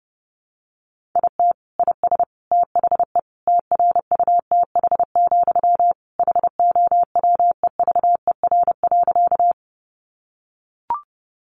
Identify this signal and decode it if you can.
Morse code, heard as a keyed tone, about 28 wpm, 700 Hz. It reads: IT IS THE TRUTH, HOWEVER.
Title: Sentences from Top 800 Words 30wpm